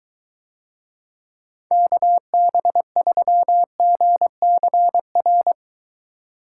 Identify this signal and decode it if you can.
Morse code, heard as a keyed tone, about 23 wpm, 700 Hz. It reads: KB3GCR